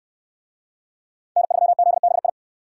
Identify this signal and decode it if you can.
Morse code, heard as a keyed tone, about 68 wpm, 700 Hz. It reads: N3ZBI